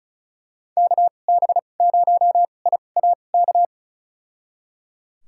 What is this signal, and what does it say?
Morse code, heard as a keyed tone, about 36 wpm, spaced as for 25 wpm, 700 Hz. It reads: KB0IAK
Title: USA Call Signs - 6 Characters - 2x Character Spacing - 35wpm